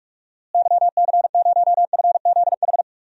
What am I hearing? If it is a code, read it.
Morse code, heard as a keyed tone, about 47 wpm, 700 Hz. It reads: YC0FZH
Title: DXCC-100 Call Signs - 6 Characters - Repeated 2x - 45wpm